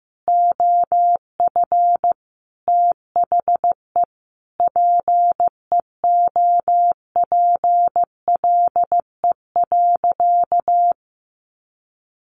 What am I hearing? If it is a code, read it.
Morse code, heard as a keyed tone, about 15 wpm, 700 Hz. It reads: OF THE PEOPLE.
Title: Sentences from Top 1500 Words 15wpm